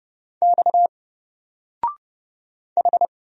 Morse code, heard as a keyed tone, about 29 wpm, 700 Hz. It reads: X H
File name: Single-Letter-Number-30wpm